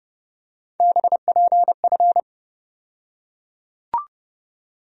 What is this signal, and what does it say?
Morse code, heard as a keyed tone, about 30 wpm, 700 Hz. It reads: BPF
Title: USA Call Sign Postfixes - 3 Letters 30wpm